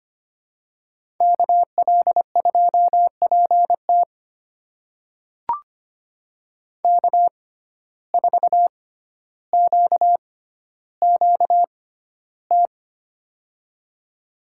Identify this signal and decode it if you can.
Morse code, heard as a keyed tone, about 25 wpm, spaced as for 13 wpm, 700 Hz. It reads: KL2PT K4QQT
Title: USA Call Signs - 5 Characters - 6x Character Spacing - 25wpm